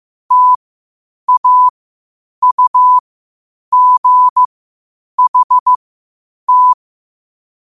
Morse code, heard as a keyed tone, about 15 wpm, spaced as for 8 wpm, 1000 Hz. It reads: TAUGHT
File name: qotdfiles / 8wpm